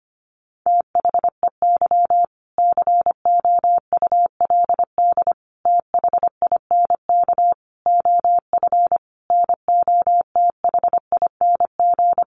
Morse code, heard as a keyed tone, about 25 wpm, 700 Hz. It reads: THEY COULD THINK OF NOTHING
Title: Sentences from Top 800 Words 25wpm